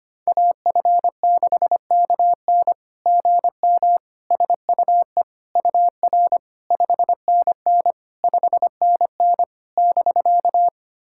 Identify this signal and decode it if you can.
Morse code, heard as a keyed tone, about 25 wpm, 700 Hz. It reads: AF6KN GM SUE UR 5NN 5NN <BK>
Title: SOTA QSO Practice 25wpm